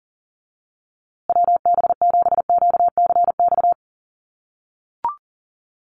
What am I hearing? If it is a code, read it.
Morse code, heard as a keyed tone, about 40 wpm, 700 Hz. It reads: WB7QCX
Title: USA Call Signs Repeated 2x 40wpm